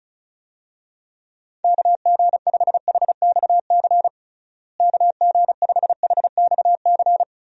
Morse code, heard as a keyed tone, about 36 wpm, 700 Hz. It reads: KG5HXC KG5HXC
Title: USA Call Signs - 6 Characters - Repeated 3x - 35wpm